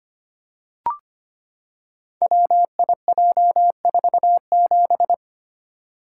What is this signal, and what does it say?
Morse code, heard as a keyed tone, about 25 wpm, 700 Hz. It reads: WIJ47
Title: Code Groups - 5 Letter:Numbers - 25wpm